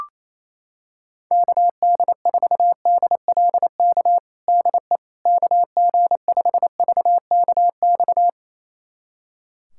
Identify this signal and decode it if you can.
Morse code, heard as a keyed tone, about 28 wpm, 700 Hz. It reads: KD4DLK DE KG5VKX
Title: USA Callsign DE Callsign - 6 Characters - 28wpm